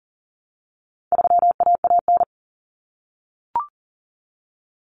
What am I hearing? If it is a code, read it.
Morse code, heard as a keyed tone, about 40 wpm, 700 Hz. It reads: 3AAN
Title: Common Cut Numbers - 4-Digit 40wpm